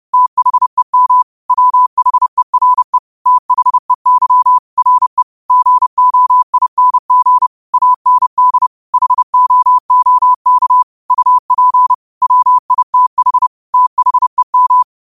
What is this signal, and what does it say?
Morse code, heard as a keyed tone, about 30 wpm, 1000 Hz. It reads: THEM WHERE THEY RE GOING AND HOOK UP WITH THEM
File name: qotdfiles / 30wpm